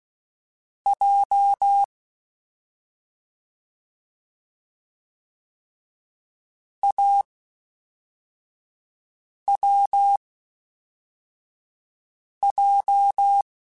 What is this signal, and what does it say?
Morse code, heard as a keyed tone, about 16 wpm, spaced as for 4 wpm, 800 Hz. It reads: J AWJ
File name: Difficult Combos / Difficult Combos-mp3s / AWJ ShortRun @5